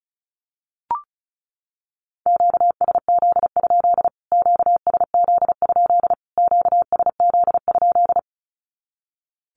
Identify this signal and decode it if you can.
Morse code, heard as a keyed tone, about 34 wpm, 700 Hz. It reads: QSZ? QSZ? QSZ?
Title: Q-Codes Repeated 3x 35wpm